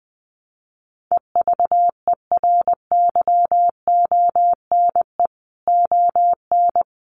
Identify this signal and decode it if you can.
Morse code, heard as a keyed tone, about 20 wpm, 700 Hz. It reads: EVERYONE ON
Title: A Sound Of Thunder 20wpm